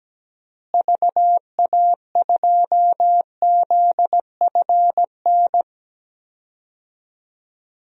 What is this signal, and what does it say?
Morse code, heard as a keyed tone, about 17 wpm, 700 Hz. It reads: VA2ZFN
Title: DXCC-100 Call Signs - 6 Characters - Repeated 3x - 17wpm